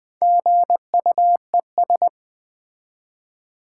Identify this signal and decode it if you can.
Morse code, heard as a keyed tone, about 20 wpm, 700 Hz. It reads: GUES
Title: Build A Word - 1 Letter At A Time - 3 to 5-Letter Words - 20wpm